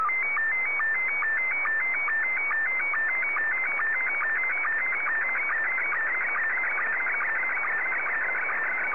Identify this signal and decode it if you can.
SSTV (slow-scan television), Scottie 1.